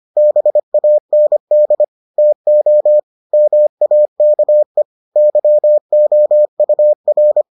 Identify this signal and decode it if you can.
Morse code, heard as a keyed tone, about 25 wpm, 600 Hz. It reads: BAND TO MAKE YOUR